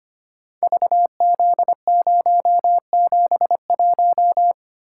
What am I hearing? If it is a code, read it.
Morse code, heard as a keyed tone, about 25 wpm, 700 Hz. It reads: VZ071
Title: Code Groups - 5 Letter:Numbers - 25wpm